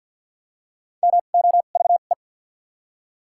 Morse code, heard as a keyed tone, about 48 wpm, spaced as for 35 wpm, 700 Hz. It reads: MOVE
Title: 2x Character Spacing - Top 200 Words - 50wpm